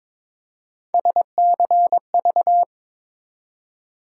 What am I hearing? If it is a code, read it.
Morse code, heard as a keyed tone, about 22 wpm, 700 Hz. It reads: SCV
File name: ARRL Sections v2 - 22wpm